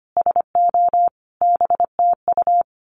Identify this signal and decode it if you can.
Morse code, heard as a keyed tone, about 25 wpm, 700 Hz. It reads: SO BTU
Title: Common QSO Phrases 25wpm